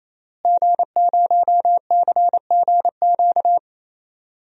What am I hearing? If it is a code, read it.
Morse code, heard as a keyed tone, about 28 wpm, 700 Hz. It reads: G0CGQ
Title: DXCC-100 Call Signs - 5 Characters - Repeated 3x - 28wpm